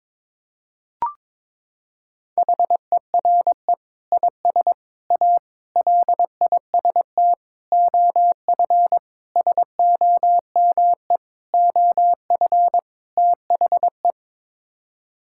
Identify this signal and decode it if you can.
Morse code, heard as a keyed tone, about 22 wpm, 700 Hz. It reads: HERE IS A LIST OF SOME OF THE
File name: practice / 22wpm / Sets of 8 Words - 22wpm